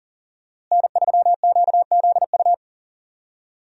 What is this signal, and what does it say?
Morse code, heard as a keyed tone, about 40 wpm, 700 Hz. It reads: N3QZU
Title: USA Call Signs - 5 Characters - Repeated 3x - 40wpm